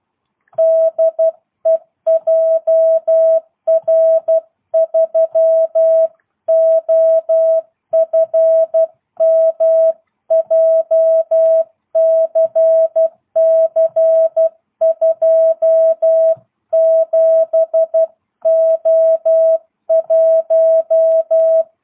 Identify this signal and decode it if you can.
Morse code, heard as a keyed tone, about 12 wpm, 660 Hz. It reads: DE JR3OFM JCC27O1
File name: jh8cbh / 2022cw